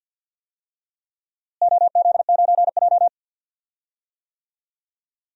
Morse code, heard as a keyed tone, about 51 wpm, 700 Hz. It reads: OZ9J
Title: DXCC-100 Call Signs - 4 Characters - Repeated 2x - 50wpm